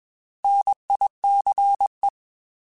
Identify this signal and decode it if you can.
Morse code, heard as a keyed tone, about 21 wpm, 800 Hz. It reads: NICE
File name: Alphabetical / Alphabetical mp3s / Ns @20